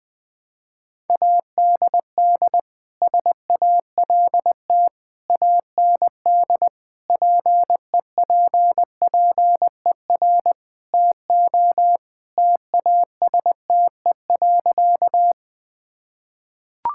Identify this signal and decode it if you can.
Morse code, heard as a keyed tone, about 21 wpm, 700 Hz. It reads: ADD SALT AND PEPPER TO TASTE.
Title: Sentences with 5-Word Phrases - 20wpm